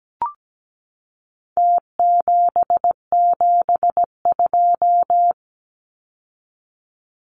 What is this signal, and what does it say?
Morse code, heard as a keyed tone, about 17 wpm, 700 Hz. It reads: T772
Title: Numbers - Contest Style - 4-Digit 17wpm